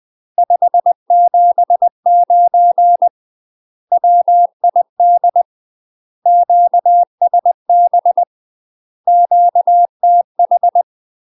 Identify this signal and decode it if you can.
Morse code, heard as a keyed tone, about 20 wpm, 700 Hz. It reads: 579 WID QSB QTH
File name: UK MASTER QSOs TRAINING / 20WPM / UK MASTER QSOs 0038